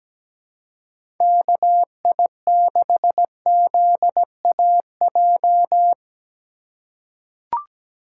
Morse code, heard as a keyed tone, about 17 wpm, 700 Hz. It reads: KI6ZAJ
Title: USA Call Signs - 6 Characters - 2x Character Spacing - 17wpm